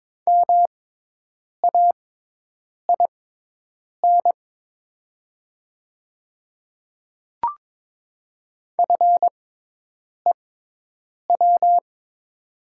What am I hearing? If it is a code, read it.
Morse code, heard as a keyed tone, about 22 wpm, spaced as for 8 wpm, 700 Hz. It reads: MAIN FEW
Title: 6x Character Spacing - Top 500 Words - 22wpm